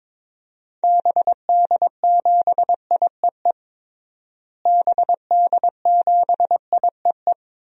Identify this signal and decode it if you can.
Morse code, heard as a keyed tone, about 21 wpm, 700 Hz. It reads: BD7IEE BD7IEE
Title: DXCC-100 Call Signs - 6 Characters - Repeated 2x - 22wpm